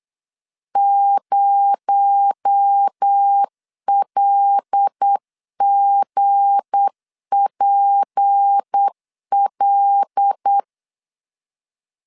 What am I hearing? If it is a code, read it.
Morse code, heard as a keyed tone, about 8 wpm, 800 Hz. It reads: 0LGPL
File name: LEZIONI / 7 GF9Z / 7 GF9Z 8wpm 3213